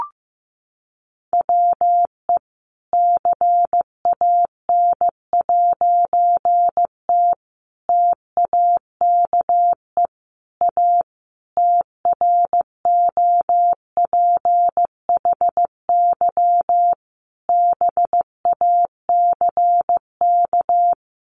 Morse code, heard as a keyed tone, about 15 wpm, 700 Hz. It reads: WE CAN'T TAKE A TROPHY BACK
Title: A Sound Of Thunder 15wpm